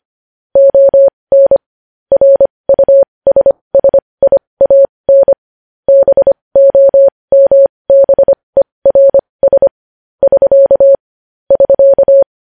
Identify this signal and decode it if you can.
Morse code, heard as a keyed tone, about 25 wpm, spaced as for 20 wpm, 550 Hz. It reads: ON RUSSIAN BOMBERS <SK> <SK>
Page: news-headlines-in-morse-code-at-20-wpm